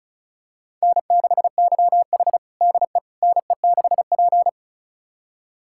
Morse code, heard as a keyed tone, about 36 wpm, 700 Hz. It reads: N6YH DE NE6P
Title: USA Callsign DE Callsign - 4 Characters - 35wpm